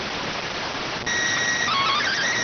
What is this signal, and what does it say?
SSTV (slow-scan television), PD 290.